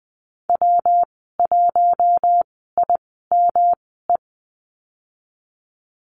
Morse code, heard as a keyed tone, about 20 wpm, spaced as for 15 wpm, 700 Hz. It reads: W1IME
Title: 122 Lesson 20g - Number 3 - Call Signs 20wpm